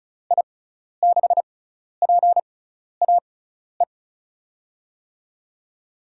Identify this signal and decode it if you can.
Morse code, heard as a keyed tone, about 36 wpm, spaced as for 12 wpm, 700 Hz. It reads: I6PAE
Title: USA Call Signs - 6 Characters - 6x Character Spacing - 35wpm